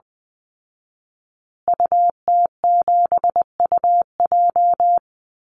Morse code, heard as a keyed tone, about 20 wpm, 700 Hz. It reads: UT7UJ
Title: Call Signs Repeated 2x - 20wpm